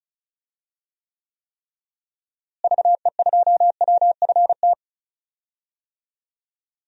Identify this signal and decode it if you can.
Morse code, heard as a keyed tone, about 34 wpm, 700 Hz. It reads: VE2WFT
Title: DXCC-100 Call Signs - 6 Characters - Repeated 3x - 35wpm